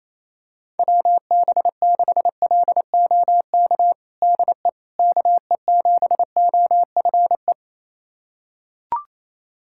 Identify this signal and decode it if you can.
Morse code, heard as a keyed tone, about 27 wpm, 700 Hz. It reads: WB6LOK DE KE7OFE
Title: USA Callsign DE Callsign - 6 Characters - 28wpm